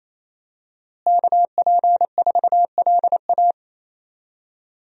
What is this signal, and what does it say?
Morse code, heard as a keyed tone, about 28 wpm, 700 Hz. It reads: KP4LA
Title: USA Call Signs - 5 Characters - Repeated 2x - 28wpm